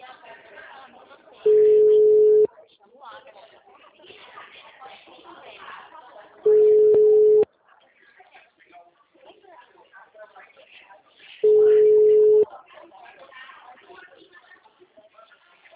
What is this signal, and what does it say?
Ringback tone (Europe, 425 Hz), with speech.
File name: out- -2009- - - .102